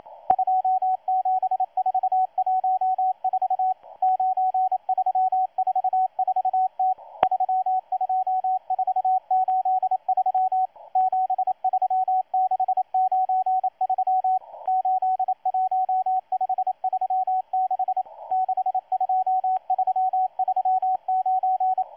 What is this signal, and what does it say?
Morse code, heard as a keyed tone, about 28 wpm, 750 Hz, with radio noise. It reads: 27414 9344T 32483 73693 81536 62339
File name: M12-2016-07-07-1310utc-13926khz-error-restart